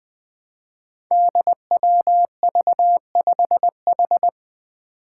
Morse code, heard as a keyed tone, about 20 wpm, 700 Hz. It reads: DWV5H
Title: Five Challenging Letter-Number Combinations 20wpm